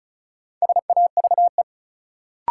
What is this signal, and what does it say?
Morse code, heard as a keyed tone, about 34 wpm, 700 Hz. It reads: SAVE
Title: Sentences from Top 1100 Words - New Word Drill - 35wpm